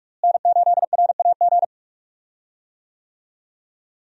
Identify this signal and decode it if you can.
Morse code, heard as a keyed tone, about 46 wpm, 700 Hz. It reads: N8RAG
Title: USA Call Signs - 5 Characters - Repeated 2x - 45wpm